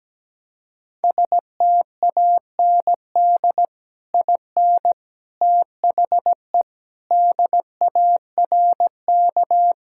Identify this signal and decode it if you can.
Morse code, heard as a keyed tone, about 17 wpm, 700 Hz. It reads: STAND IN THE DARK
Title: Sentences from Top 1600 Words 17wpm